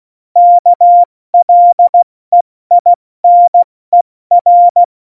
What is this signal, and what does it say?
Morse code, heard as a keyed tone, about 16 wpm, 700 Hz. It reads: KLEINER
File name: Morsekurs / Witze / Witz 28. August 2018, 70 F80